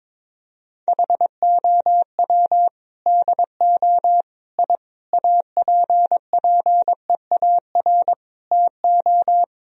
Morse code, heard as a keyed tone, about 22 wpm, 700 Hz. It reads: HOW DO I APPEAR TO
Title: Sentences from Top 1000 Words - Review Part 2 - 22wpm